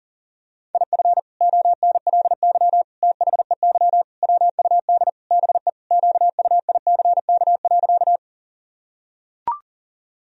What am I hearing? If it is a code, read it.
Morse code, heard as a keyed tone, about 40 wpm, 700 Hz. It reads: IF ONLY THEY WUD BE QUICK.
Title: Sentences from Top 500 Words and Abbreviations 40wpm